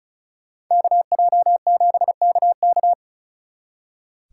Morse code, heard as a keyed tone, about 47 wpm, spaced as for 41 wpm, 700 Hz. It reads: KJ7KK
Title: Speed Racing - USA 2x2 Call Signs 35wpm